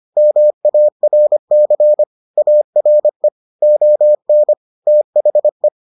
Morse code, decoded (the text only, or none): MARC ARE ON THE